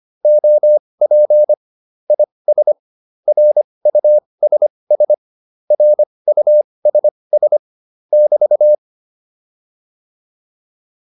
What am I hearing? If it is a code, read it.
Morse code, heard as a keyed tone, about 25 wpm, spaced as for 20 wpm, 600 Hz. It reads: OP IS RUSS RUSS <BT>